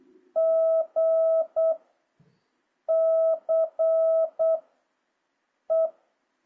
Morse code, heard as a keyed tone, about 8 wpm, 650 Hz. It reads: G C E
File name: jh8cbh / 2026cw